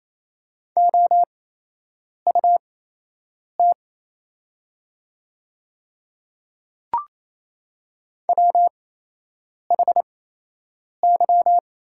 Morse code, heard as a keyed tone, about 28 wpm, spaced as for 8 wpm, 700 Hz. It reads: OUT WHY
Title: 8x Character Spacing - Top 200 Words - 28wpm